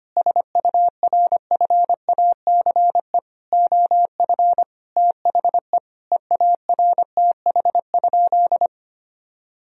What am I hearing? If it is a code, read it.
Morse code, heard as a keyed tone, about 25 wpm, 700 Hz. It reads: SURFACE OF THE EARTH?
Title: Sentences from Top 1000 Words - Review Part 3 - 25wpm